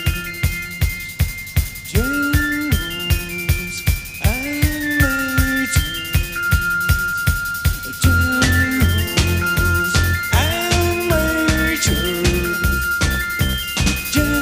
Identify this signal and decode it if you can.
Dial-up modem handshake.